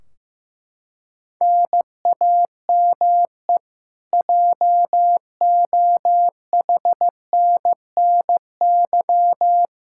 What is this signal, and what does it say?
Morse code, heard as a keyed tone, about 15 wpm, 700 Hz. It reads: NAME JOHNNY
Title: Practice QSO - Exchange 1 - Beginning - 15wpm